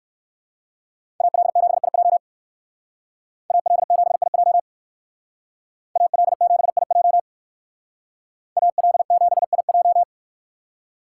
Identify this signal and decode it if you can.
Morse code, heard as a keyed tone, about 58 wpm, 700 Hz. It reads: AL7IJ AL7IJ AL7IJ AL7IJ